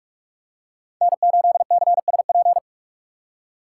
Morse code, heard as a keyed tone, about 46 wpm, 700 Hz. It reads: N8CSP
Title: USA Call Signs - 5 Characters - 4x Character Spacing - 45wpm